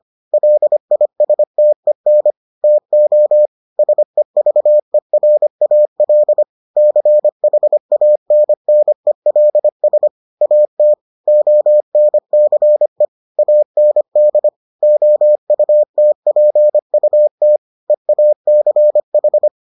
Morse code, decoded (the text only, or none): LISTEN TO SEVERAL CHANNELS AT ONCE AND OUTPUT EACH